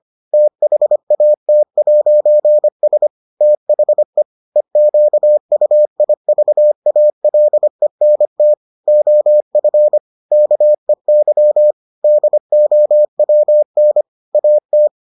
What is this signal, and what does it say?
Morse code, heard as a keyed tone, about 25 wpm, 600 Hz. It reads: THAT'S THE EQUIVALENT OF KEY DOWN AT